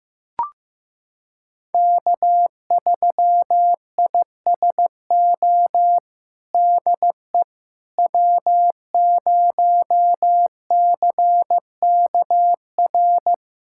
Morse code, heard as a keyed tone, about 15 wpm, 700 Hz. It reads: K3ISO DE W0CKR
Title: USA Callsign DE Callsign - 5 Characters - 15wpm